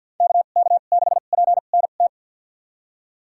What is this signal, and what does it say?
Morse code, heard as a keyed tone, about 46 wpm, spaced as for 35 wpm, 700 Hz. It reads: KK6PNT
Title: USA Call Signs - 2x Character Spacing - 50wpm